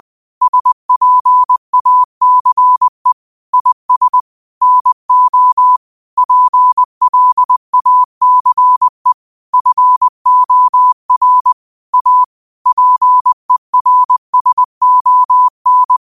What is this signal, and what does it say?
Morse code, heard as a keyed tone, about 20 wpm, 1000 Hz. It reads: SPACE IS NO PLACE FOR A PERSON